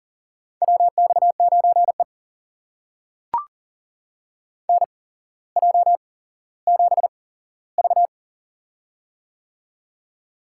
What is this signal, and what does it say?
Morse code, heard as a keyed tone, about 40 wpm, spaced as for 17 wpm, 700 Hz. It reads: WX9E NJ7V
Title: USA Call Signs - 4 Characters - 8x Character Spacing - 40wpm